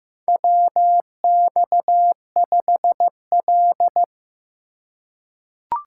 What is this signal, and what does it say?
Morse code, heard as a keyed tone, about 15 wpm, 700 Hz. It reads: WX5L
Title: USA Call Signs - 4 Characters - 2x Character Spacing - 15wpm